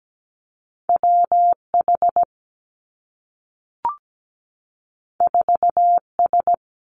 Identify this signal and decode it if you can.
Morse code, heard as a keyed tone, about 17 wpm, 700 Hz. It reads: WH 4S
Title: Two Challenging Letter-Number Combinations 17wpm